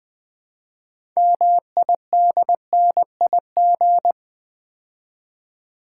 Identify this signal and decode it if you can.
Morse code, heard as a keyed tone, about 20 wpm, 690 Hz. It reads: MIDNIG
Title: Build A Word - 3 Letters At A Time - 8-Letter Words - 20wpm